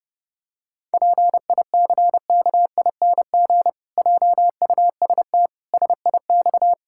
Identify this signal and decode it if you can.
Morse code, heard as a keyed tone, about 30 wpm, 700 Hz. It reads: PICKING JUST SIX
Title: Pangram Challenge v2 - 30wpm